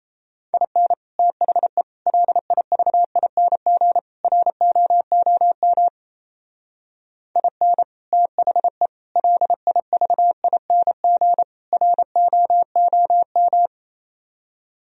Morse code, heard as a keyed tone, about 34 wpm, 700 Hz. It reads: IN THE LIVING ROOM IN THE LIVING ROOM
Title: Speed-Racing - Sets of 4 Words 28wpm